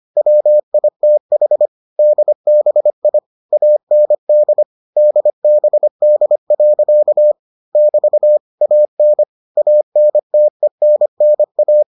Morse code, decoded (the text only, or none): WITH DBI AND DBD. <BT> AN ANTENNA